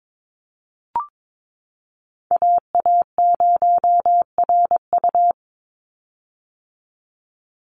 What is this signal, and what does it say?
Morse code, heard as a keyed tone, about 22 wpm, 700 Hz. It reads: AA0RU
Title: USA Call Signs 22wpm